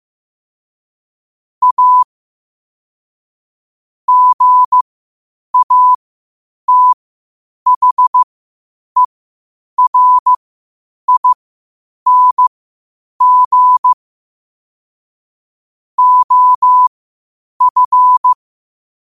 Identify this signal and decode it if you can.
Morse code, heard as a keyed tone, about 15 wpm, spaced as for 8 wpm, 1000 Hz. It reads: A GATHERING OF
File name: qotdfiles / 8wpm